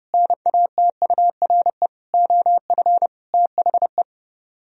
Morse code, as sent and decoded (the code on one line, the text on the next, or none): -. .- - ..- .-. / --- ..-. / - .... .
NATURE OF THE